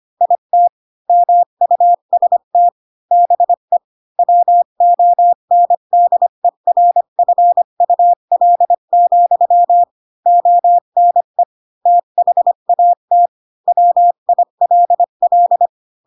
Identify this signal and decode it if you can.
Morse code, heard as a keyed tone, about 25 wpm, 700 Hz. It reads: IT MUST BE WONDERFUL, ONE THAT WILL